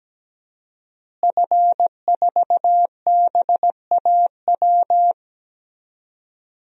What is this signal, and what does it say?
Morse code, heard as a keyed tone, about 17 wpm, 700 Hz. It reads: F4BAW
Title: DXCC-100 Call Signs - 5 Characters - Repeated 3x - 17wpm